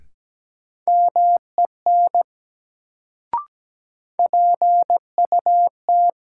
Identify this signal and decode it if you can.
Morse code, heard as a keyed tone, about 17 wpm, 700 Hz. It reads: MEN PUT
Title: Sentences from Top 200 Words - New Word Drill - 17wpm